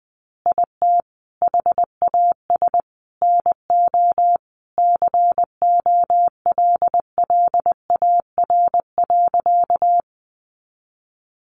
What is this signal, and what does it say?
Morse code, heard as a keyed tone, about 19 wpm, 700 Hz. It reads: IT HAS NO COLLAR.
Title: Easy Phrases 20wpm